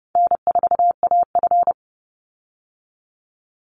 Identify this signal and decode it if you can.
Morse code, heard as a keyed tone, about 30 wpm, 700 Hz. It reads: N4AF
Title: Call Signs - 30wpm